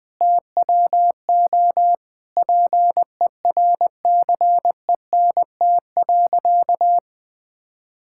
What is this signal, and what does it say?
Morse code, heard as a keyed tone, about 20 wpm, 700 Hz. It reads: TWO PERCENT.
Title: Sentences with 8-Word Phrases - Spoken Beginning - 20wpm